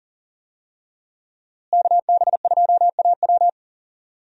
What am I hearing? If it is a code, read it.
Morse code, heard as a keyed tone, about 40 wpm, 700 Hz. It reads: KB2AW
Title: USA Call Signs - 5 Characters - Repeated 3x - 40wpm